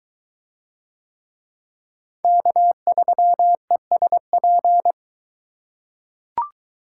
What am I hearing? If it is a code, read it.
Morse code, heard as a keyed tone, about 20 wpm, 700 Hz. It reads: K3ESP